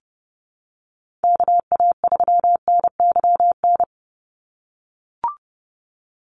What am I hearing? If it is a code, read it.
Morse code, heard as a keyed tone, about 28 wpm, 700 Hz. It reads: KA3NYN
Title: USA Call Signs 30wpm